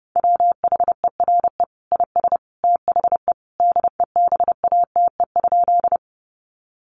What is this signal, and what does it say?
Morse code, heard as a keyed tone, about 30 wpm, 700 Hz. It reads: WHERE IS THE DEBATE?
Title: Sentences from Top 1100 Words 30wpm